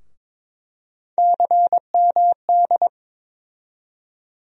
Morse code, heard as a keyed tone, about 22 wpm, 700 Hz. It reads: CMD
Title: USA Call Sign Postfixes - 3 Letters 22wpm